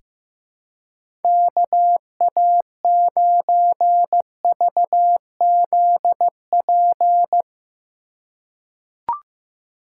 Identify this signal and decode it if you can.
Morse code, heard as a keyed tone, about 15 wpm, 700 Hz. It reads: KA9VZP
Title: USA Call Signs - 6 Characters - 2x Character Spacing - 15wpm